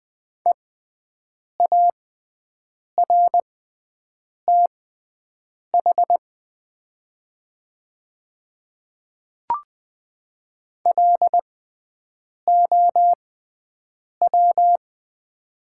Morse code, heard as a keyed tone, about 20 wpm, spaced as for 7 wpm, 700 Hz. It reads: EARTH LOW
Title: 6x Character Spacing - Top 200 Words - 20wpm